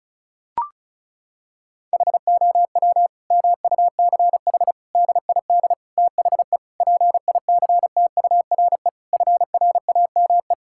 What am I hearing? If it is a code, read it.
Morse code, heard as a keyed tone, about 36 wpm, 700 Hz. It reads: HOW MUCH DID THE PICTURE FRAME
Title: Sentences from Top 1000 Words - Review Part 6 - 35wpm